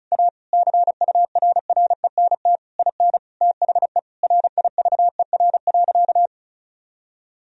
Morse code, heard as a keyed tone, about 36 wpm, 700 Hz. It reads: A CURRENT IN THE RIVER.
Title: Sentences from Top 1000 Words - Review Part 4 - 35wpm